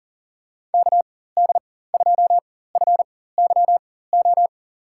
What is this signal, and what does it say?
Morse code, heard as a keyed tone, about 43 wpm, spaced as for 19 wpm, 690 Hz. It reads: KD2FYO
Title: USA Call Signs - 4x Character Spacing - 40wpm